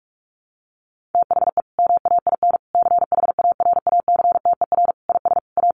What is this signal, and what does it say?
Morse code, heard as a keyed tone, about 44 wpm, 700 Hz. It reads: THE MAIN CHARACTER IS A
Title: Additional Easy Phrases 45wpm